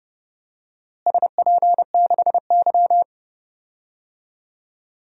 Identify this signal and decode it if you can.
Morse code, heard as a keyed tone, about 30 wpm, 700 Hz. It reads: SP6Y
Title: DXCC-100 Call Signs - 4 Characters - Repeated 2x - 30wpm